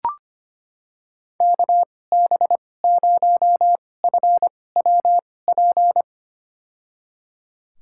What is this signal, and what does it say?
Morse code, heard as a keyed tone, about 25 wpm, spaced as for 18 wpm, 700 Hz. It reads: KB0FWP
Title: USA Call Signs - 6 Characters - 2x Character Spacing - 25wpm